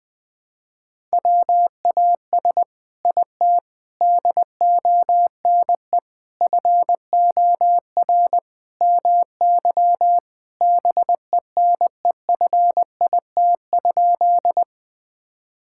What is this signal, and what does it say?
Morse code, heard as a keyed tone, about 20 wpm, 700 Hz. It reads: WAS IT DONE FOR MY BENEFIT?
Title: Sentences from Top 1000 Words - Review Part 6 - 20wpm